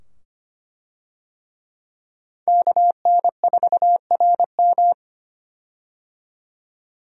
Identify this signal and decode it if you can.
Morse code, heard as a keyed tone, about 25 wpm, 700 Hz. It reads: KN4RM
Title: USA Call Signs - 5 Characters - 25wpm